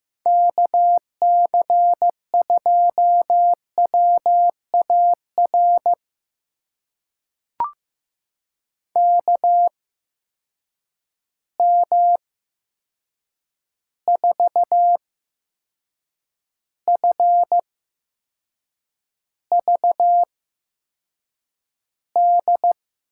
Morse code, heard as a keyed tone, about 15 wpm, spaced as for 6 wpm, 700 Hz. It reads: KC2WAR KM4FVD
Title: USA Call Signs - 6 Characters - 8x Character Spacing - 15wpm